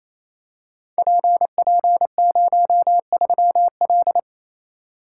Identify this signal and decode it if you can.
Morse code, heard as a keyed tone, about 28 wpm, 700 Hz. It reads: PP03L
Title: Code Groups - 5 Letter:Numbers - 28wpm